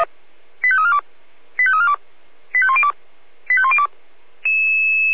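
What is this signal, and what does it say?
Five-tone selective call (ZVEI), heard as quick runs of five tones, tones of about 80 ms. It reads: 85342 85342 85102 85102
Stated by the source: ZVEI five-tone alert received by SDR.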